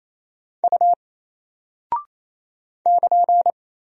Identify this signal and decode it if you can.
Morse code, heard as a keyed tone, about 28 wpm, 700 Hz. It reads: U <KN>
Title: Single Character (Letter:Number:Punctuation:Prosign) Alt 28wpm